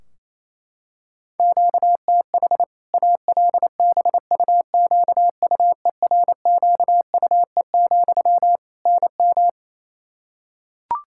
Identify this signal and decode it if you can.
Morse code, heard as a keyed tone, about 27 wpm, 700 Hz. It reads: QTH ALBUQUERQUE, NM
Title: Practice QSO - Exchange 1 - Beginning - 28wpm